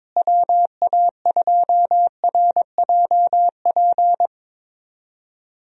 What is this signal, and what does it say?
Morse code, heard as a keyed tone, about 22 wpm, 700 Hz. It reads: WA2RJP
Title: USA Call Signs - 6 Characters - Repeated 2x - 22wpm